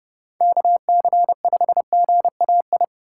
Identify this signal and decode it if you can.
Morse code, heard as a keyed tone, about 30 wpm, 700 Hz. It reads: KC5GAI